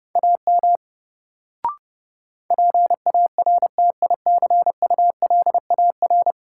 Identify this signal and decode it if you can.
Morse code, heard as a keyed tone, about 32 wpm, 700 Hz. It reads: AM PARTICULAR
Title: Sentences from Top 600 Words - New Word Drill - 30wpm